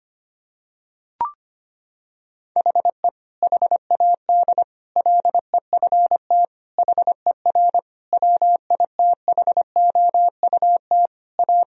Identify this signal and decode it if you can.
Morse code, heard as a keyed tone, about 25 wpm, 700 Hz. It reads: HE HAD LEFT HER WITHOUT A
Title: Sentences from Top 1000 Words - Review Part 3 - 25wpm